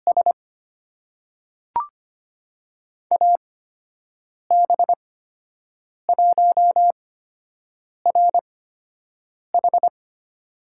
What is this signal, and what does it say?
Morse code, heard as a keyed tone, about 25 wpm, spaced as for 7 wpm, 700 Hz. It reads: S AB1RH
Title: USA Call Signs - 5 Characters - 8x Character Spacing - 25wpm